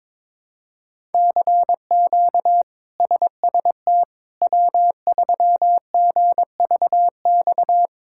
Morse code, heard as a keyed tone, about 21 wpm, 700 Hz. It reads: CQ SST W3GVX
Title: K1USN SST QSO Practice - 22wpm